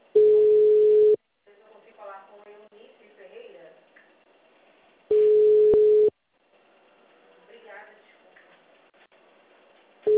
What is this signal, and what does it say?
Ringback tone (Europe, 425 Hz), with speech.